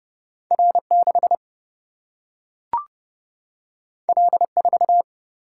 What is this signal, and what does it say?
Morse code, heard as a keyed tone, about 30 wpm, 700 Hz. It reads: R 6 L 4
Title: Single-Letter Single-Number 30wpm